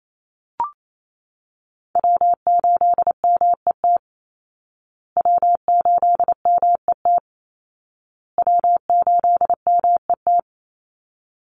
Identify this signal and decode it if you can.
Morse code, heard as a keyed tone, about 28 wpm, 700 Hz. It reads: W8MET W8MET W8MET
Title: Call Signs Repeated 2x - 28wpm